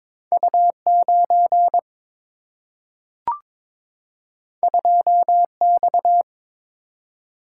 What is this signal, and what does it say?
Morse code, heard as a keyed tone, about 22 wpm, 700 Hz. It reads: U9 2X
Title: Single-Letter Single-Number 22wpm